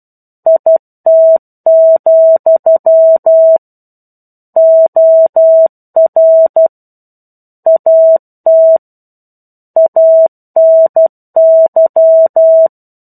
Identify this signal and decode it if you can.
Morse code, heard as a keyed tone, about 12 wpm, 650 Hz. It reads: IT, OR AT ANY
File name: 01-Chapter-01 12 WPM